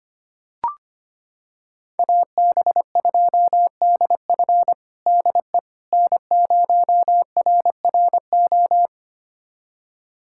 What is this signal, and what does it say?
Morse code, heard as a keyed tone, about 25 wpm, 700 Hz. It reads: AB2DF DE N0RRO
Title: USA Callsign DE Callsign 25wpm